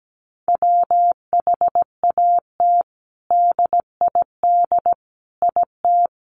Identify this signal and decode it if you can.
Morse code, heard as a keyed tone, about 17 wpm, 700 Hz. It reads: WHAT DID IT
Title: Sentences from Top 1400 Words 17wpm